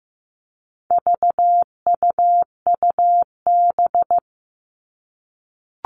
Morse code, heard as a keyed tone, about 15 wpm, 700 Hz. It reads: VUUB
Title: All Cut Numbers - 4-Digit 15wpm